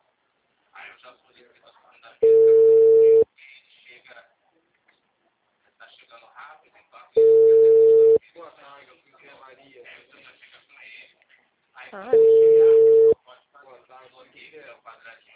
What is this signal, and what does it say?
Ringback tone (China, 450 Hz), with speech.